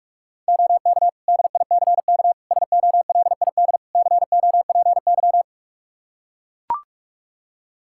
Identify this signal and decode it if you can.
Morse code, heard as a keyed tone, about 44 wpm, 700 Hz. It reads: OK DICK SOLID COPY
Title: Practice QSO - Exchange 3 - More Info - 45wpm